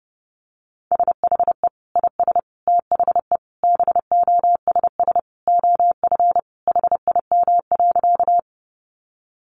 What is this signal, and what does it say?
Morse code, heard as a keyed tone, about 30 wpm, 700 Hz. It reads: SHE IS THE BOSS OF HIM.
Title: Sentences from Top 700 Words 30wpm